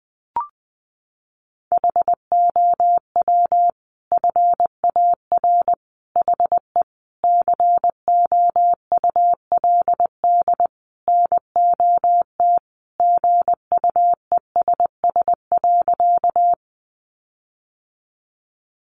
Morse code, heard as a keyed tone, about 20 wpm, 700 Hz. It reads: HOW FAR HE COULD NOT GUESS.
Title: Sentences from Top 1500 Words 20wpm